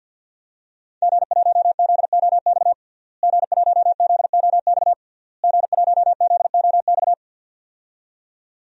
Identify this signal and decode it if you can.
Morse code, heard as a keyed tone, about 46 wpm, 700 Hz. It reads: G1ZOX G1ZOX G1ZOX
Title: DXCC-100 Call Signs - 5 Characters - Repeated 3x - 50wpm